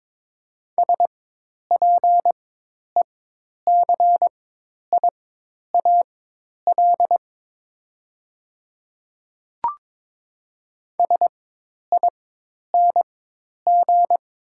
Morse code, heard as a keyed tone, about 22 wpm, spaced as for 10 wpm, 700 Hz. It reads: SPECIAL SING
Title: 4x Character Spacing - Top 500 Words - 22wpm